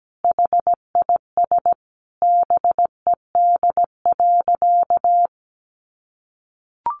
Morse code, heard as a keyed tone, about 17 wpm, 700 Hz. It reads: HIS BED.
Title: Sentences from Top 500 Words 17wpm